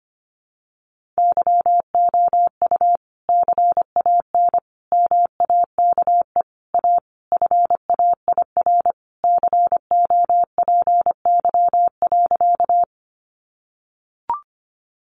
Morse code, heard as a keyed tone, about 25 wpm, 700 Hz. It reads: YOU CAN MAKE A FAIR COPY.
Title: Sentences from Top 1400 Words 25wpm